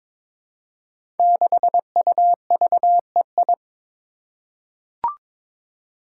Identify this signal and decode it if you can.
Morse code, heard as a keyed tone, about 22 wpm, 700 Hz. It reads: 6UVEI
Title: Five Challenging Letter-Number Combinations 22wpm